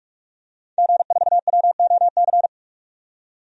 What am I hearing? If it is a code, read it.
Morse code, heard as a keyed tone, about 44 wpm, 700 Hz. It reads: G4WOC
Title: DXCC-100 Call Signs - 5 Characters - 45wpm